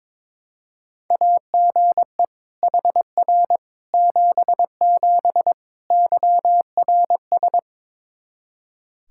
Morse code, heard as a keyed tone, about 22 wpm, 700 Hz. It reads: AGE HR 77 YRS
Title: Practice QSO - Exchange 3 - More Info - 22wpm